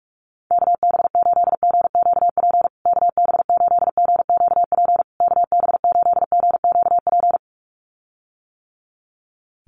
Morse code, heard as a keyed tone, about 44 wpm, 690 Hz. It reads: KB8GQP KB8GQP KB8GQP
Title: USA Call Signs Repeated 3x 45wpm